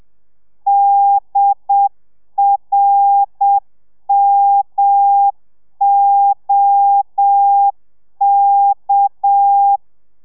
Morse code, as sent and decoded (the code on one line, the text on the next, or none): -.. .-. -- --- -.-
DRMOK